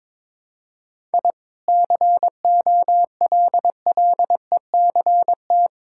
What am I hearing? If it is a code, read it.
Morse code, heard as a keyed tone, about 22 wpm, 700 Hz. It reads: I COLLECT
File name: practice / 22wpm / Easy Sentences with Hints - 22wpm